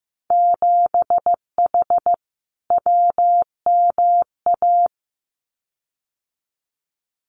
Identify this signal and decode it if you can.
Morse code, heard as a keyed tone, about 15 wpm, 700 Hz. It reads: 7H WMA
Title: Winter Field Day Exchanges Encore 15wpm